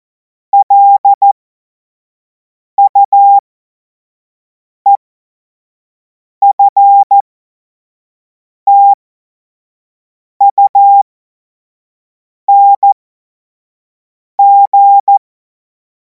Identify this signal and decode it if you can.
Morse code, heard as a keyed tone, about 14 wpm, spaced as for 5 wpm, 800 Hz. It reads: LUEFTUNG
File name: Morsekurs / Witze / Witz 07. Juli 2026 25 F70